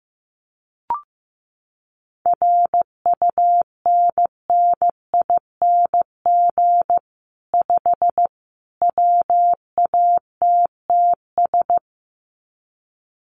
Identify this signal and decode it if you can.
Morse code, heard as a keyed tone, about 15 wpm, 700 Hz. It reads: RUNNING 5 WATTS
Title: Common QSO Phrases 15wpm